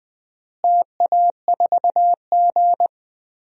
Morse code, heard as a keyed tone, about 20 wpm, 700 Hz. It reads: TA4G
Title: DXCC-100 Call Signs - 4 Characters - Repeated 3x - 20wpm